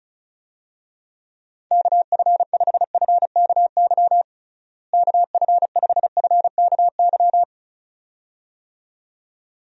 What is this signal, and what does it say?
Morse code, heard as a keyed tone, about 36 wpm, 700 Hz. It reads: KF5FKY KF5FKY
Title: USA Call Signs - 6 Characters - Repeated 2x - 35wpm